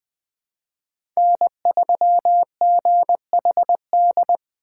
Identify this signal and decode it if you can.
Morse code, heard as a keyed tone, about 20 wpm, 700 Hz. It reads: N3GHD
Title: USA Call Signs - 5 Characters - 4x Character Spacing - 20wpm